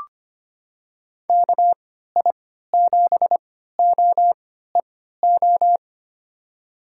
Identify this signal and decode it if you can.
Morse code, heard as a keyed tone, about 25 wpm, spaced as for 14 wpm, 700 Hz. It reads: KI7OEO
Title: USA Call Signs - 6 Characters - 3x Character Spacing - 25wpm